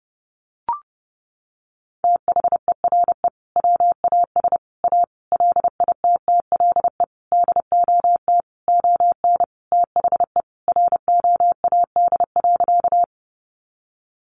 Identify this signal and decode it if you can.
Morse code, heard as a keyed tone, about 30 wpm, 700 Hz. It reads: THERE WAS A LITTLE DOT ON THE ROAD.
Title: Sentences from Top 1500 Words 30wpm